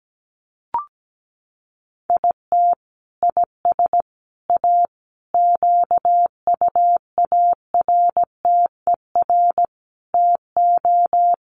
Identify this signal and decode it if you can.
Morse code, heard as a keyed tone, about 17 wpm, 700 Hz. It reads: IT IS A QUARTER TO
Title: Sentences from Top 1300 Words 17wpm